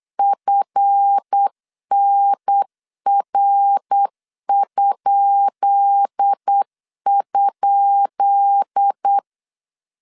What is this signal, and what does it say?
Morse code, heard as a keyed tone, about 8 wpm, 800 Hz. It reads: FNR??
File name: LEZIONI / 10 SIGNS / 10B INT 8wpm 3534